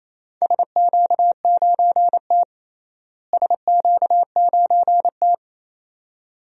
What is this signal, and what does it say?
Morse code, heard as a keyed tone, about 28 wpm, 700 Hz. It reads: SQ9T SQ9T
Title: DXCC-100 Call Signs - 4 Characters - Repeated 2x - 28wpm